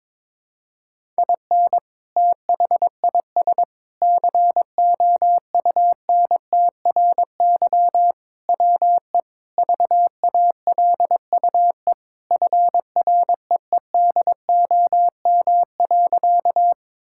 Morse code, heard as a keyed tone, about 22 wpm, 700 Hz. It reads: IN THIS COUNTRY WE VALUE FREEDOM.
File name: practice / 22wpm / Sentences with 3-Word Phrases - 22wpm